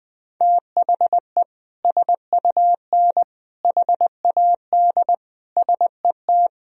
Morse code, heard as a keyed tone, about 21 wpm, 700 Hz. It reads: THE SUN HAD SET
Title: Sentences from Top 300 Words and Abbreviations 20wpm